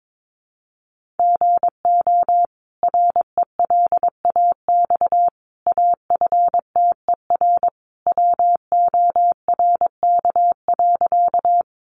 Morse code, heard as a keyed tone, about 22 wpm, 700 Hz. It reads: GO RELAX AFTER WORK.
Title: Sentences from Top 1400 Words 22wpm